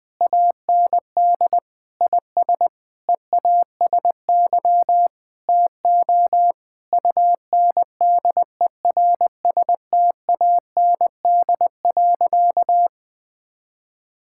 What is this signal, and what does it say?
Morse code, heard as a keyed tone, about 20 wpm, 700 Hz. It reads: AND IS EASY TO UNDERSTAND.
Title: Sentences from Top 1000 Words - Review Part 6 - 20wpm